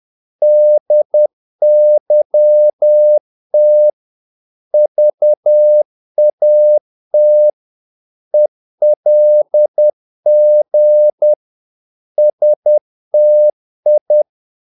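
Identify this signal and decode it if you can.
Morse code, heard as a keyed tone, about 10 wpm, 600 Hz. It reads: DYT VAT ELG STI
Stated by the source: Morse practice of short Danish words at 10 wpm, from CW med Gnister.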